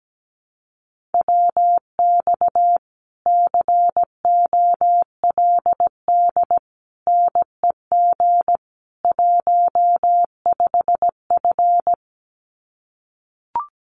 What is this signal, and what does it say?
Morse code, heard as a keyed tone, about 17 wpm, 700 Hz. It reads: WX COLD NEG 15F
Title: Common QSO Phrases Encore 17wpm